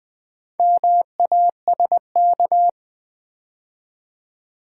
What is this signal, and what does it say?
Morse code, heard as a keyed tone, about 20 wpm, 700 Hz. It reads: MASK
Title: Build A Word - 2 Letters At A Time - 4-Letter Words - 20wpm